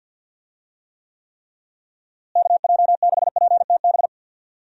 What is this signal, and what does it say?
Morse code, heard as a keyed tone, about 46 wpm, 700 Hz. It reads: KJ6PTB
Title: USA Call Signs - 6 Characters - Repeated 3x - 50wpm